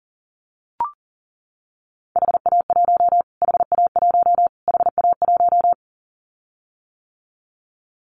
Morse code, heard as a keyed tone, about 40 wpm, 700 Hz. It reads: HA1 HA1 HA1
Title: Callsign Prefixes for Top 25 DXCC - Repeated 3x - 40wpm